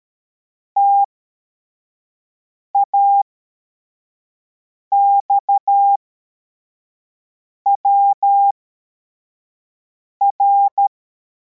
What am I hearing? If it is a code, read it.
Morse code, heard as a keyed tone, about 13 wpm, spaced as for 4 wpm, 800 Hz. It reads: TAXWR
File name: formation / cw / Text64